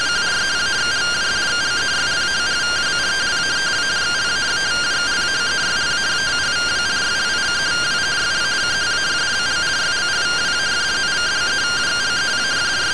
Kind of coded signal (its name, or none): two-tone teleprinter signal